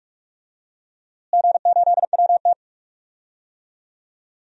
Morse code, heard as a keyed tone, about 41 wpm, 700 Hz. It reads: G8WT
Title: DXCC-100 Call Signs - 4 Characters - Repeated 3x - 45wpm